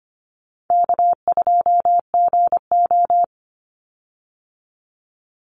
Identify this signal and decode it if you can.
Morse code, heard as a keyed tone, about 25 wpm, 700 Hz. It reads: K2GO
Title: Call Signs - 25wpm